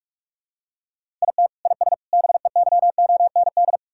Morse code, heard as a keyed tone, about 41 wpm, 700 Hz. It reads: IT IS BEYOND
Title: Sentences from Top 1000 Words - Review Part 6 - 45wpm